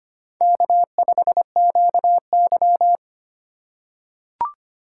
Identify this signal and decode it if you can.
Morse code, heard as a keyed tone, about 25 wpm, 700 Hz. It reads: K5QY
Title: USA Call Signs - 4 Characters - 4x Character Spacing - 25wpm